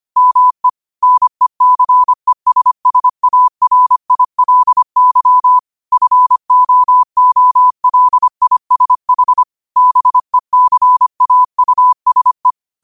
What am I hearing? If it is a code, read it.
Morse code, heard as a keyed tone, about 25 wpm, 1000 Hz. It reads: ME NECESSARILY FOOLISH BECAUSE
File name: qotdfiles / 25wpm